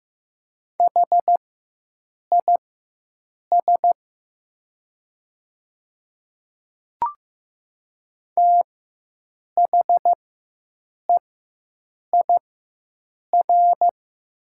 Morse code, heard as a keyed tone, about 15 wpm, spaced as for 7 wpm, 700 Hz. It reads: HIS THEIR
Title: 4x Character Spacing - Top 500 Words - 15wpm